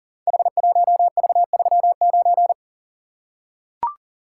Morse code, heard as a keyed tone, about 40 wpm, 700 Hz. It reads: H1V39
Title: Code Groups - 5 Letter:Numbers - 40wpm